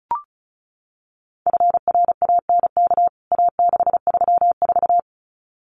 Morse code, heard as a keyed tone, about 34 wpm, 700 Hz. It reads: FRANK A634
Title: CWT Exchange 35wpm